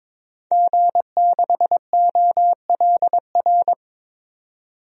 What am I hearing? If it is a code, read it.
Morse code, heard as a keyed tone, about 22 wpm, 700 Hz. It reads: G6OLR
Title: DXCC-100 Call Signs - 5 Characters - 22wpm